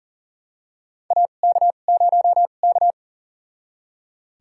Morse code, heard as a keyed tone, about 38 wpm, spaced as for 28 wpm, 700 Hz. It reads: AK0K
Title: USA Call Signs - 4 Characters - 2x Character Spacing - 40wpm